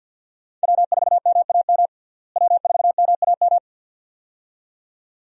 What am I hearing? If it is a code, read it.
Morse code, heard as a keyed tone, about 50 wpm, 700 Hz. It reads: W4MAM W4MAM
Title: USA Call Signs - 5 Characters - Repeated 2x - 50wpm